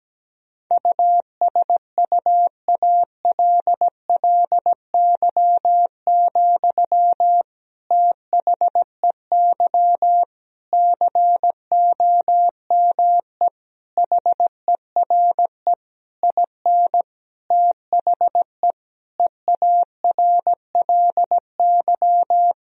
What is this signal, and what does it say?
Morse code, heard as a keyed tone, about 17 wpm, 700 Hz. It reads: USUALLY, THEY COME HERE IN THE EARLY
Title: Sentences from Top 1000 Words - Review Part 3 - 17wpm